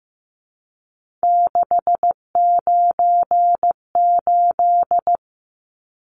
Morse code, heard as a keyed tone, about 15 wpm, 700 Hz. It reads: 698
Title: Numbers - 3-Digit 15wpm